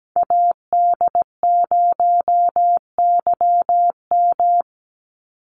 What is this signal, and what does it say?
Morse code, heard as a keyed tone, about 17 wpm, 700 Hz. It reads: AD0YM
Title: NA SOTA Chaser Callsigns 17wpm